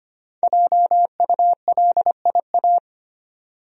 Morse code, heard as a keyed tone, about 25 wpm, 700 Hz. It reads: JULIA
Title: USA Common Names - 25wpm